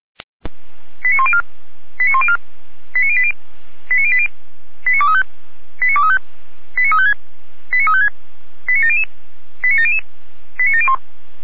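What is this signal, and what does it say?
Five-tone selective call (ZVEI), heard as quick runs of five tones, tones of about 70 ms. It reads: 89104 89104 89080 89080 89235 89235 89356 89356 89700 89700 89701
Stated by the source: ZVEI five-tone call 89094 received by SDR, channel 455.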